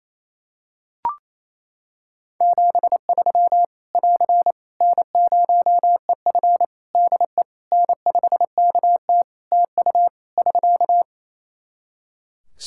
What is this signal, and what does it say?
Morse code, heard as a keyed tone, about 28 wpm, 700 Hz. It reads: 73 <AR> N0EF DE N5KT TU <SK>
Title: Practice QSO - Exchange 4 - Ending - 28wpm